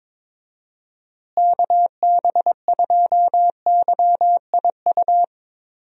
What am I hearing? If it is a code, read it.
Morse code, heard as a keyed tone, about 23 wpm, 700 Hz. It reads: KB2YIU
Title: USA Call Signs - 6 Characters - Repeated 3x - 22wpm